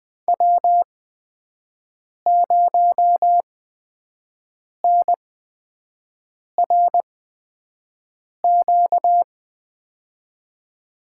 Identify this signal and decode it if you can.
Morse code, heard as a keyed tone, about 20 wpm, spaced as for 5 wpm, 700 Hz. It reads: W0NRQ
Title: USA Call Signs - 5 Characters - 8x Character Spacing - 20wpm